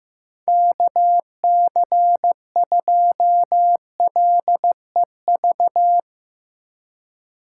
Morse code, heard as a keyed tone, about 15 wpm, 700 Hz. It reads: KC2LEV
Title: USA Call Signs - 4x Character Spacing - 15wpm